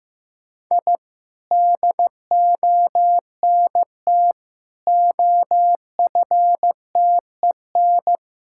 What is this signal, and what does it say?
Morse code, heard as a keyed tone, about 15 wpm, 700 Hz. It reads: I DONT OFTEN
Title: Sentences from Top 1000 Words - Review Part 6 - 15wpm